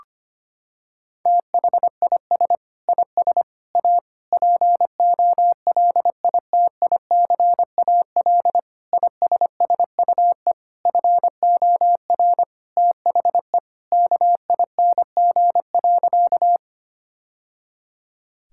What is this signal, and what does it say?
Morse code, heard as a keyed tone, about 25 wpm, 700 Hz. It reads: THIS IS A POLITICAL ISSUE FOR THE KING.
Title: Sentences from Top 1000 Words - Review Part 6 - 25wpm